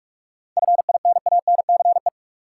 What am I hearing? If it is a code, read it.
Morse code, heard as a keyed tone, about 46 wpm, 700 Hz. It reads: FINANCE
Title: K1USN SST QSO Practice 45wpm